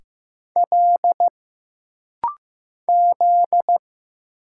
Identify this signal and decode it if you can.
Morse code, heard as a keyed tone, about 15 wpm, 700 Hz. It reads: L Z
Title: Single Letters - Rapid-Fire 15wpm